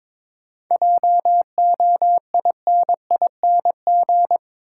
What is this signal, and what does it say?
Morse code, heard as a keyed tone, about 21 wpm, 700 Hz. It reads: JOINING
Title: Sets of 5 Words - New Word Drill - 22wpm